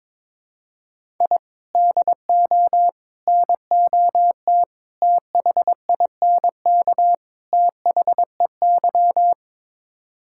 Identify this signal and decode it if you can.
Morse code, heard as a keyed tone, about 22 wpm, 700 Hz. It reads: I DO NOT THINK THEY
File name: practice / 22wpm / Build A Phrase - 5-Word Phrases - 22wpm